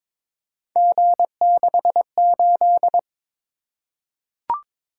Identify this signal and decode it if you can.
Morse code, heard as a keyed tone, about 22 wpm, 700 Hz. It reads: G68
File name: practice / 22wpm / Three Letter-Number Combinations - 22wpm